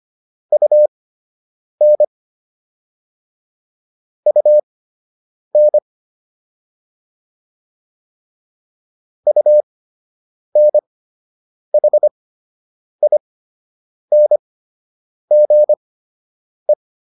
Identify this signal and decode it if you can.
Morse code, heard as a keyed tone, about 25 wpm, spaced as for 8 wpm, 600 Hz. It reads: UN UN UNHINGE